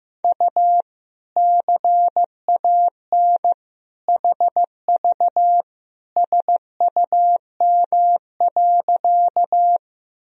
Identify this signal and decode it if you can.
Morse code, heard as a keyed tone, about 15 wpm, 700 Hz. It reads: U CAN HV SUM.
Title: Sentences from Top 100 Words and Abbreviations 15wpm